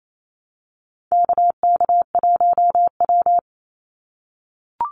Morse code, heard as a keyed tone, about 28 wpm, 700 Hz. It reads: KK1W
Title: NA SOTA Activator Callsigns 28wpm